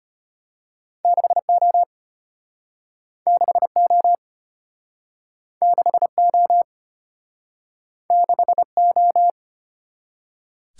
Morse code, heard as a keyed tone, about 33 wpm, 700 Hz. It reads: 6O 6O 6O 6O